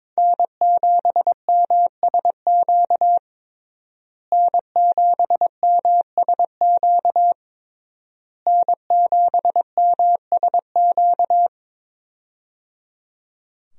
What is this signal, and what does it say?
Morse code, heard as a keyed tone, about 22 wpm, 700 Hz. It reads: N7MSQ N7MSQ N7MSQ
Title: USA Call Signs - 5 Characters - Repeated 3x - 22wpm